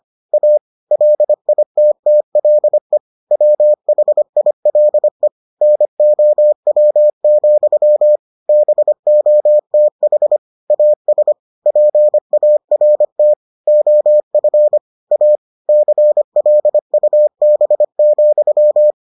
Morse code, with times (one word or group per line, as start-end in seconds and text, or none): 0.3-0.6 A
0.9-3.0 LITTLE
3.3-5.3 WHILE
5.6-8.2 NOW,
8.5-10.4 BOTH
10.7-11.3 AS
11.7-13.3 PART
13.7-14.8 OF
15.1-15.4 A
15.7-18.9 CLUB,